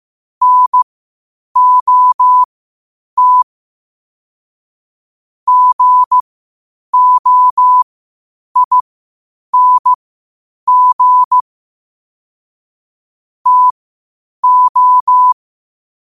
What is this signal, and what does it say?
Morse code, heard as a keyed tone, about 15 wpm, spaced as for 8 wpm, 1000 Hz. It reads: NOT GOING TO
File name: qotdfiles / 8wpm